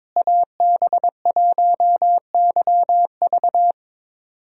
Morse code, heard as a keyed tone, about 22 wpm, 700 Hz. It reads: AB1YV
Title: USA Call Signs - 5 Characters - Repeated 2x - 22wpm